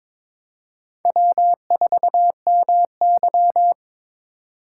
Morse code, heard as a keyed tone, about 26 wpm, 700 Hz. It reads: W4MY
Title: Speed Racing - USA 1x2 Call Signs 22wpm